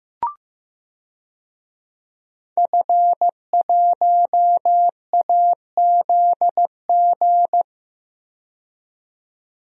Morse code, heard as a keyed tone, about 15 wpm, 700 Hz. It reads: F1AZG
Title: DXCC-100 Call Signs - 5 Characters - 15wpm